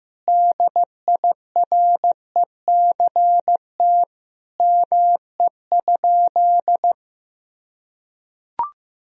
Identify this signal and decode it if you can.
Morse code, heard as a keyed tone, about 15 wpm, 700 Hz. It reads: DIRECT ME?
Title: Sentences from Top 1000 Words - Review Part 2 - 15wpm